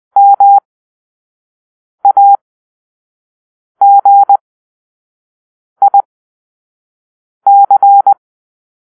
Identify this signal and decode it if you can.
Morse code, heard as a keyed tone, about 20 wpm, spaced as for 5 wpm, 800 Hz. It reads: MAGIC
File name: arrl / Grp-4 Drill-4